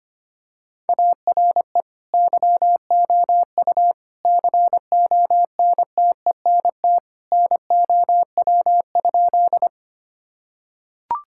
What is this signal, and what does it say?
Morse code, heard as a keyed tone, about 25 wpm, 700 Hz. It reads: ARE YOU CONTENT NOW?
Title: Sentences from Top 1000 Words - Review Part 6 - 25wpm